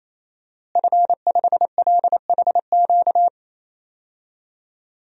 Morse code, heard as a keyed tone, about 27 wpm, 700 Hz. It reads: F5LHQ
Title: DXCC-100 Call Signs - 5 Characters - Repeated 3x - 28wpm